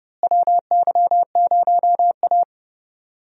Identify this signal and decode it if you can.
Morse code, heard as a keyed tone, about 30 wpm, 700 Hz. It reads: WY0A
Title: USA Call Signs - 4 Characters - Repeated 2x - 30wpm